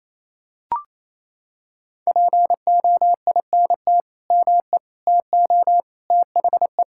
Morse code, heard as a keyed tone, about 28 wpm, 700 Hz. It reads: POINT ME TO THE
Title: Sentences from Top 1600 Words 28wpm